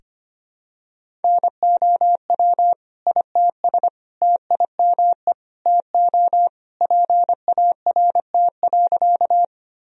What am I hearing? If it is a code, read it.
Morse code, heard as a keyed tone, about 25 wpm, 700 Hz. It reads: NOW ITS TIME TO PART.
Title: Sentences from Top 1000 Words - Review Part 1 - 25wpm